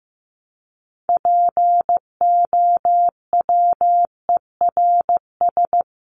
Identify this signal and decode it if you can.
Morse code, heard as a keyed tone, about 15 wpm, 700 Hz. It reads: POWERS
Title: ES:S Suffix - Top 100 Words with Base Words 15wpm